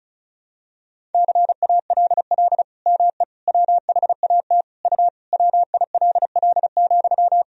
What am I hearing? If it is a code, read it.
Morse code, heard as a keyed tone, about 36 wpm, 700 Hz. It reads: CALL ME WHAT U WILL,
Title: Sentences from Top 200 Words and Abbreviations 35wpm